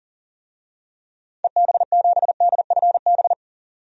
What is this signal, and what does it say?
Morse code, heard as a keyed tone, about 40 wpm, 700 Hz. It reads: EB7DFB
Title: DXCC-100 Call Signs - 6 Characters - 40wpm